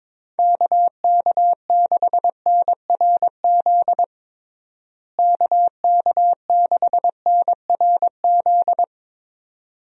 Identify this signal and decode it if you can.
Morse code, heard as a keyed tone, about 22 wpm, 700 Hz. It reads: KK6NRZ KK6NRZ
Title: USA Call Signs - 6 Characters - Repeated 2x - 22wpm